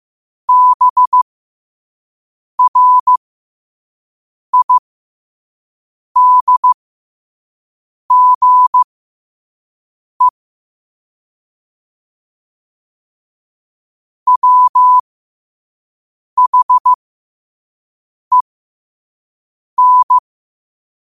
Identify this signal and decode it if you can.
Morse code, heard as a keyed tone, about 15 wpm, spaced as for 5 wpm, 1000 Hz. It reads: BRIDGE WHEN
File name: qotdfiles / 5wpm